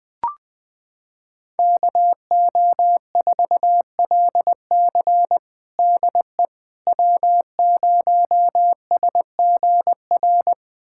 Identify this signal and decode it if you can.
Morse code, heard as a keyed tone, about 20 wpm, 700 Hz. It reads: KO4LC DE W0SGR
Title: USA Callsign DE Callsign - 5 Characters - 20wpm